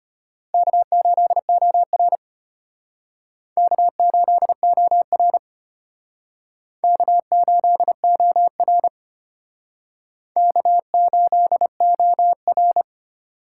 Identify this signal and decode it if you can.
Morse code, heard as a keyed tone, about 32 wpm, 700 Hz. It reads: K8OR K8OR K8OR K8OR